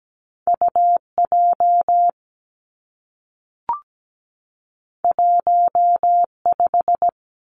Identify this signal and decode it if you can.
Morse code, heard as a keyed tone, about 17 wpm, 700 Hz. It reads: UJ 15
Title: Two Challenging Letter-Number Combinations 17wpm